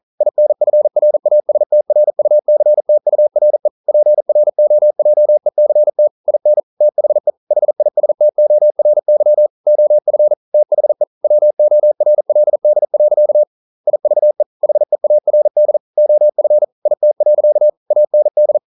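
Morse code, heard as a keyed tone, about 41 wpm, 600 Hz. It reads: INFRASTRUCTURE PROJECT IN THE HISTORY OF THE WORLD. IVE HEARD OF IT. AND
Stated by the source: Morse code at 40 wpm, practice text, CW med Gnister.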